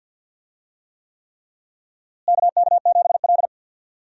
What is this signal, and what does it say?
Morse code, heard as a keyed tone, about 49 wpm, 700 Hz. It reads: KK7L
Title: USA Call Signs - 4 Characters - Repeated 3x - 50wpm